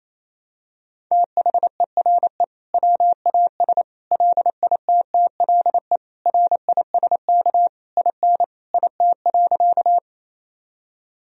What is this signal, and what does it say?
Morse code, heard as a keyed tone, about 28 wpm, 700 Hz. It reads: THERE WAS LITTLE RISK IN IT.
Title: Sentences from Top 1000 Words - Review Part 4 - 28wpm